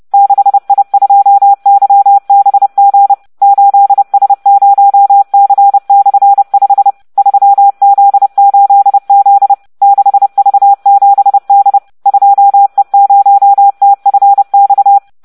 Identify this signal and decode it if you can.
Morse code, heard as a keyed tone, about 30 wpm, 800 Hz. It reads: 6I2YBG 8S0C/5 3Z8Z 6V7D 2E0TFX